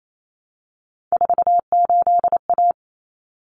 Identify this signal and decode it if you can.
Morse code, heard as a keyed tone, about 28 wpm, 700 Hz. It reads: 48A
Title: Common Cut Numbers - 3-Digit 28wpm